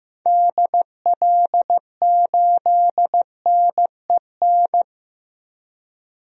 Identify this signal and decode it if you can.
Morse code, heard as a keyed tone, about 16 wpm, 700 Hz. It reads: DL8NEN
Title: DXCC-100 Call Signs - 6 Characters - Repeated 3x - 15wpm